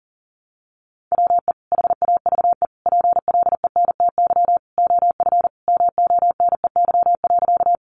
Morse code, heard as a keyed tone, about 40 wpm, 700 Hz. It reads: WE HAVE PLENTY OF MONEY.
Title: Sentences from Top 1300 Words 40wpm